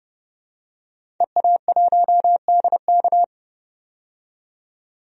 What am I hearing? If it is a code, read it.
Morse code, heard as a keyed tone, about 32 wpm, 700 Hz. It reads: EA1DK
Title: DXCC-100 Call Signs - 5 Characters - 30wpm